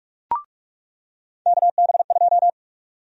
Morse code, heard as a keyed tone, about 41 wpm, 700 Hz. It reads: KB2
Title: USA Call Sign Prefixes - 45wpm